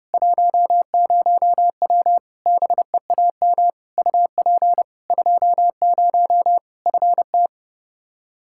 Morse code, heard as a keyed tone, about 30 wpm, 700 Hz. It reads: 10W BEAM UP 20 FT
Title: Practice QSO - Exchange 2 - Info - 30wpm